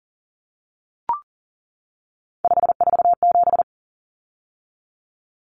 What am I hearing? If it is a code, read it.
Morse code, heard as a keyed tone, about 40 wpm, 700 Hz. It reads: 547
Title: Numbers - 3-Digit 40wpm